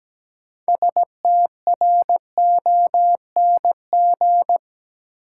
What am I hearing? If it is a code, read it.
Morse code, heard as a keyed tone, about 17 wpm, 700 Hz. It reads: STRONG
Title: Sentences from Top 500 Words - New Word Drill - 17wpm